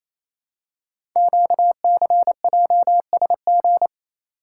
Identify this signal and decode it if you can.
Morse code, heard as a keyed tone, about 27 wpm, 700 Hz. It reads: QCJSG
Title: Code Groups - 5 Letters - 28wpm